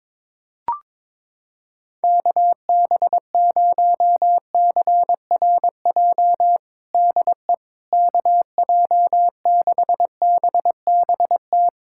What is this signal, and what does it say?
Morse code, heard as a keyed tone, about 21 wpm, 700 Hz. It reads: KB0CRJ DE KJ6BBT
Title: USA Callsign DE Callsign - 6 Characters - 22wpm